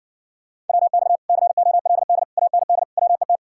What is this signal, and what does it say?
Morse code, heard as a keyed tone, about 69 wpm, 700 Hz. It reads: WX COLD AND WET